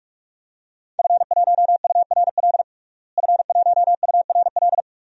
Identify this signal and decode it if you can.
Morse code, heard as a keyed tone, about 46 wpm, 700 Hz. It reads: F1URL F1URL
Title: DXCC-100 Call Signs - 5 Characters - Repeated 3x - 45wpm